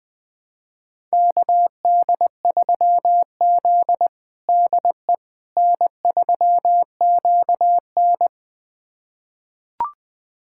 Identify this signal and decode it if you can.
Morse code, heard as a keyed tone, about 20 wpm, 700 Hz. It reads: KD3Z DE N3QN
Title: USA Callsign DE Callsign - 4 Characters - 20wpm